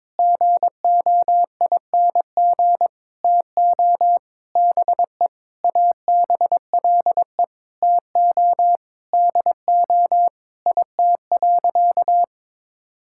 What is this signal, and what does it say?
Morse code, heard as a keyed tone, about 22 wpm, 700 Hz. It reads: GOING TO BE ABLE TO DO IT.
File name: practice / 22wpm / Sentences with 4-Word Phrases - 22wpm